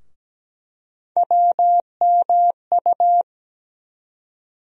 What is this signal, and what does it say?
Morse code, heard as a keyed tone, about 17 wpm, 700 Hz. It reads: WMU
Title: USA Call Sign Postfixes - 3 Letters 17wpm